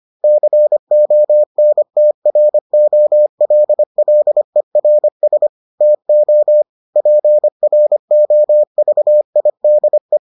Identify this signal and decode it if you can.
Morse code, heard as a keyed tone, about 25 wpm, 600 Hz. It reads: CONTROLLERS TO PROVIDE